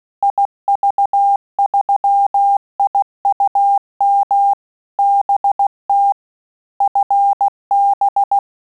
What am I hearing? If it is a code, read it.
Morse code, heard as a keyed tone, about 16 wpm, 800 Hz. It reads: IV3IUM BT FB
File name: QSO CW 75 CPM